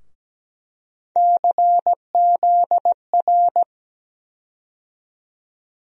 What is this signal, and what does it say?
Morse code, heard as a keyed tone, about 17 wpm, 700 Hz. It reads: CZR
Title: USA Call Sign Postfixes - 3 Letters 17wpm